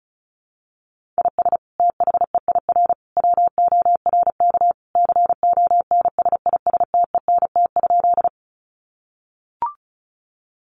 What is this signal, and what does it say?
Morse code, heard as a keyed tone, about 34 wpm, 700 Hz. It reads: IS THEIR WORK CONSISTENT?
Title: Sentences from Top 1500 Words 35wpm